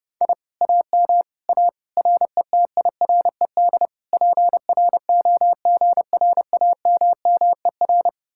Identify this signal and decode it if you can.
Morse code, heard as a keyed tone, about 33 wpm, 700 Hz. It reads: I AM A RETIRED PROGRAMMER